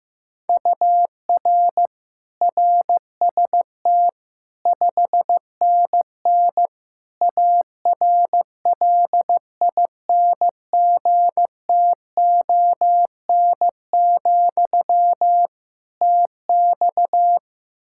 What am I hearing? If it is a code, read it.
Morse code, heard as a keyed tone, about 16 wpm, 700 Hz. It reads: UR RST 5NN ARLINGTON, TX
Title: Practice QSO - Exchange 1 - Beginning - 15wpm